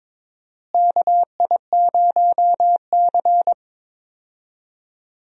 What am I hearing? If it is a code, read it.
Morse code, heard as a keyed tone, about 21 wpm, 700 Hz. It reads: KI0C
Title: USA Call Signs - 4 Characters - Repeated 2x - 22wpm